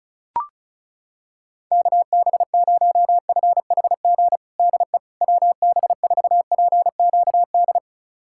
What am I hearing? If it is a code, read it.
Morse code, heard as a keyed tone, about 36 wpm, 700 Hz. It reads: KB0FHG DE WB4PQD
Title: USA Callsign DE Callsign - 6 Characters - 35wpm